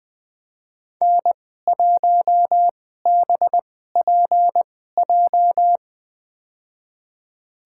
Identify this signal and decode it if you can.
Morse code, heard as a keyed tone, about 20 wpm, spaced as for 14 wpm, 700 Hz. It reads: N1BPJ
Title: USA Call Signs - 2x Character Spacing - 20wpm